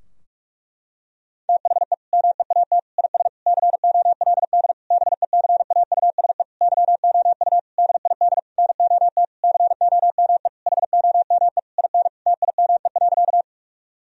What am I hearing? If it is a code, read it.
Morse code, heard as a keyed tone, about 46 wpm, 700 Hz. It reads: THE MEAT IS COLD BECAUSE YOU DID NOT COME HOME IN TIME.
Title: Sentences from Top 1000 Words - Review Part 3 - 45wpm